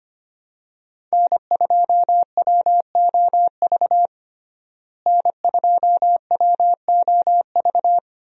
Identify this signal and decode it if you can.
Morse code, heard as a keyed tone, about 25 wpm, 700 Hz. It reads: N2WOV N2WOV
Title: USA Call Signs - 5 Characters - Repeated 2x - 25wpm